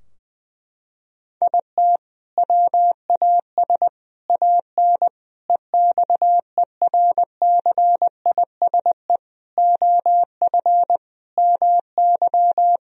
Morse code, heard as a keyed tone, about 20 wpm, 700 Hz. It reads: IT WAS AN EXERCISE OF MY
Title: Sentences from Top 1000 Words - Review Part 4 - 20wpm